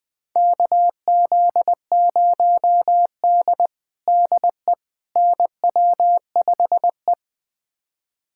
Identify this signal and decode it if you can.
Morse code, heard as a keyed tone, about 20 wpm, 700 Hz. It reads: KZ0D DE NW5E
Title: USA Callsign DE Callsign - 4 Characters - 20wpm